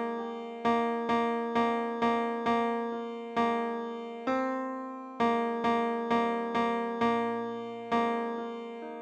There is no code.